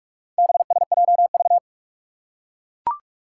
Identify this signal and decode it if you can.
Morse code, heard as a keyed tone, about 44 wpm, 700 Hz. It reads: BSJV
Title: Four Challenging Letter-Number Combinations 45wpm